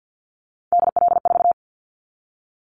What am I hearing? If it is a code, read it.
Morse code, heard as a keyed tone, about 50 wpm, 700 Hz. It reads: DL4
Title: Callsign Prefixes for Top 50 DXCC - 50wpm